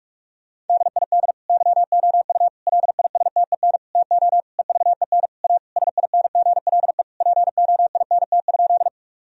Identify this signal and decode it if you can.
Morse code, heard as a keyed tone, about 42 wpm, 700 Hz. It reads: DID YOU LISTEN TO EVEN A SINGLE POINT?
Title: Sentences from Top 1000 Words - Review Part 4 - 45wpm